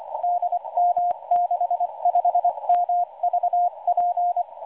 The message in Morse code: DM65MVP